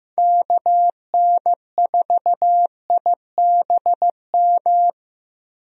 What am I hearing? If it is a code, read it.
Morse code, heard as a keyed tone, about 15 wpm, 700 Hz. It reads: KN4IBM
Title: USA Call Signs - 6 Characters - Repeated 2x - 15wpm